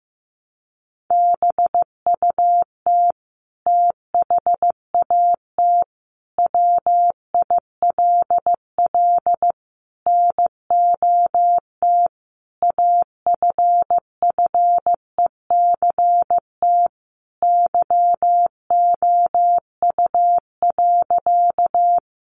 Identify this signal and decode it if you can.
Morse code, heard as a keyed tone, about 15 wpm, 700 Hz. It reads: BUT THAT WILL NOT AFFECT YOU.
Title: Sentences from Top 900 Words 15wpm